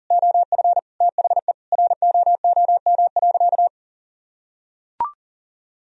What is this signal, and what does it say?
Morse code, heard as a keyed tone, about 40 wpm, 700 Hz. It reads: OF THE ROOM.
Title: Sentences from Top 500 Words and Abbreviations 40wpm